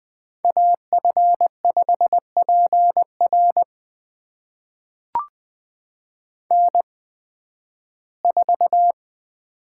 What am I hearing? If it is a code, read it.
Morse code, heard as a keyed tone, about 20 wpm, spaced as for 13 wpm, 700 Hz. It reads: AF5PR N4
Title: USA Call Signs - 5 Characters - 8x Character Spacing - 20wpm